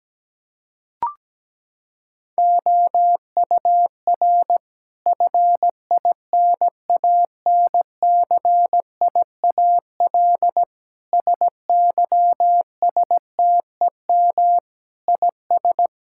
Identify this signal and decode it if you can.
Morse code, heard as a keyed tone, about 17 wpm, 700 Hz. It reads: OUR FINANCIAL SYSTEM IS
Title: Sentences from Top 1000 Words - Review Part 5 - 17wpm